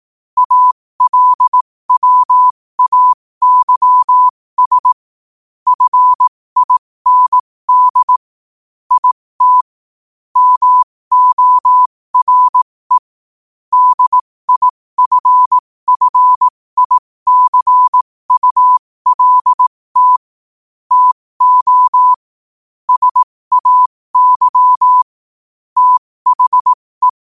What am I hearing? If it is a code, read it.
Morse code, heard as a keyed tone, about 18 wpm, spaced as for 15 wpm, 1000 Hz. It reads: ALWAYS FIND IT MORE DIFFICULT TO SAY THE
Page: 15 WPM morse code quotes